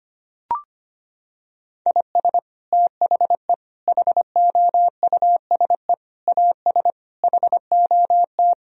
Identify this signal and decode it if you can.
Morse code, heard as a keyed tone, about 25 wpm, 700 Hz. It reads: IS THE HOUSE AS HOT
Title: Sentences from Top 1000 Words - Review Part 4 - 25wpm